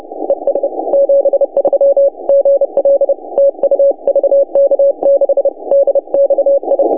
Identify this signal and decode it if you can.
Morse code, heard as a keyed tone, about 30 wpm, 560 Hz. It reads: ES 73 GL TU VK6DXI
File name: vk6dxi 160m s50u